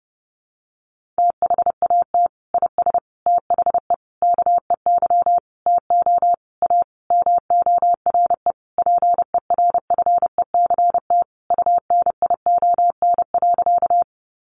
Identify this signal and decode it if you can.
Morse code, heard as a keyed tone, about 30 wpm, 700 Hz. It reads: THAT IS THE KEY TO A MORE PERFECT UNION.
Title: Sentences from Top 1400 Words 30wpm